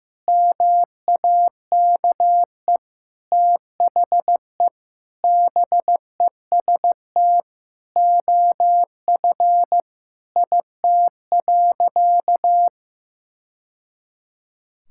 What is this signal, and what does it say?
Morse code, heard as a keyed tone, about 15 wpm, 700 Hz. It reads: MAKE THE BEST OF IT.
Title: Sentences from Top 1000 Words - Review Part 5 - 15wpm